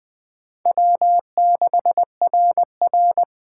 Morse code, heard as a keyed tone, about 21 wpm, 700 Hz. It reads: W6RR
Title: USA Call Signs - 4 Characters - 3x Character Spacing - 20wpm